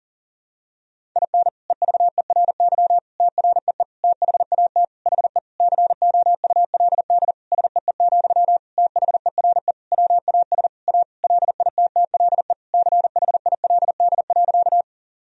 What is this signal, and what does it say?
Morse code, heard as a keyed tone, about 40 wpm, 700 Hz. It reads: IN EVERY TREE THAT HE COULD SEE, THERE WAS A LITTLE CHILD.
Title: Sentences from Top 1000 Words - Review Part 1 - 40wpm